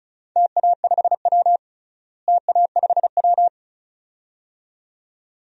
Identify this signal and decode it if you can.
Morse code, heard as a keyed tone, about 36 wpm, 690 Hz. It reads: TA5W TA5W
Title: DXCC-100 Call Signs - 4 Characters - Repeated 2x - 35wpm